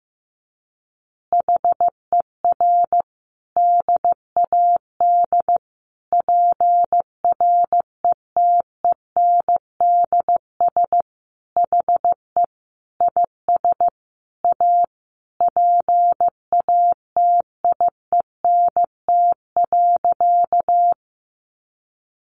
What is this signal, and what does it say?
Morse code, heard as a keyed tone, about 15 wpm, 700 Hz. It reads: HER DAD PRETENDS HE IS A PATIENT.
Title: Additional Easy Phrases 15wpm